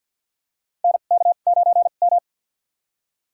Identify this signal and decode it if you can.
Morse code, heard as a keyed tone, about 53 wpm, spaced as for 37 wpm, 700 Hz. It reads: NK9M
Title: USA Call Signs - 4 Characters - 2x Character Spacing - 50wpm